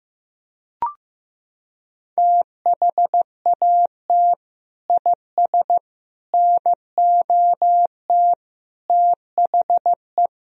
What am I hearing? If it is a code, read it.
Morse code, heard as a keyed tone, about 15 wpm, 700 Hz. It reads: THAT IS NOT THE
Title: Sentences from Top 1000 Words - Review Part 2 - 15wpm